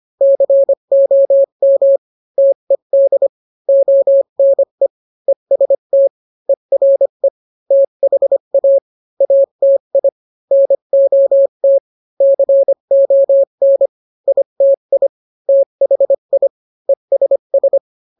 Morse code, heard as a keyed tone, about 25 wpm, 550 Hz. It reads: COM TED ONE EST ERE THA ATI NOT CON ITI THI ESS